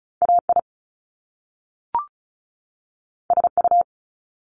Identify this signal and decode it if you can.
Morse code, heard as a keyed tone, about 36 wpm, 700 Hz. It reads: AI SU
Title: Top Two Letter Combinations in Words 35wpm